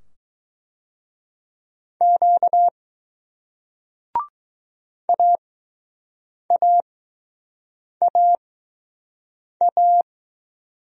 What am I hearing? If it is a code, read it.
Morse code, heard as a keyed tone, about 20 wpm, spaced as for 6 wpm, 700 Hz. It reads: Q AAAA